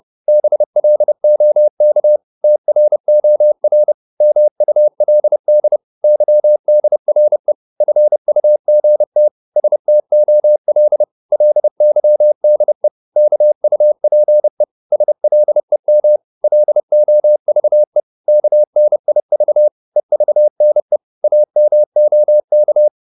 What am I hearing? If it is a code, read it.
Morse code, heard as a keyed tone, about 30 wpm, 600 Hz. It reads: BLOK TROR MULD YDRE FUGT STOL LYDE KUPE SLEM LOVE KNIV EVNE AMOK
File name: Korte ord DK 30wpm